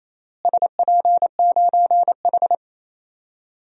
Morse code, heard as a keyed tone, about 28 wpm, 700 Hz. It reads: SP9H
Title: DXCC-100 Call Signs - 4 Characters - Repeated 3x - 28wpm